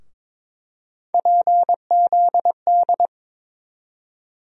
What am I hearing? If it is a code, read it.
Morse code, heard as a keyed tone, about 22 wpm, 700 Hz. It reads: PZD
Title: USA Call Sign Postfixes - 3 Letters 22wpm